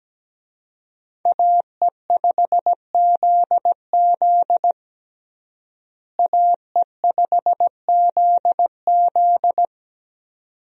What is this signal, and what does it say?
Morse code, heard as a keyed tone, about 17 wpm, 700 Hz. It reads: AE5ZZ AE5ZZ
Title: USA Call Signs - 5 Characters - Repeated 2x - 17wpm